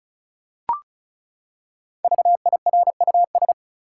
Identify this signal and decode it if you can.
Morse code, heard as a keyed tone, about 36 wpm, 700 Hz. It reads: VIRUS
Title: Five Letter Words - 35wpm